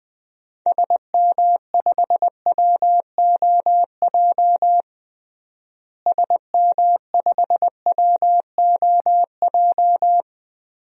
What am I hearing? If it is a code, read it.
Morse code, heard as a keyed tone, about 21 wpm, 700 Hz. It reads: SM5WOJ SM5WOJ
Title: DXCC-100 Call Signs - 6 Characters - Repeated 3x - 20wpm